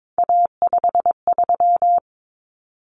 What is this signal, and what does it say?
Morse code, heard as a keyed tone, about 22 wpm, 700 Hz. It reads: A53
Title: Common Cut Numbers - 3-Digit 22wpm